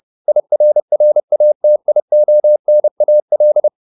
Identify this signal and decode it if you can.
Morse code, heard as a keyed tone, about 30 wpm, 600 Hz. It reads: IRRATIONAL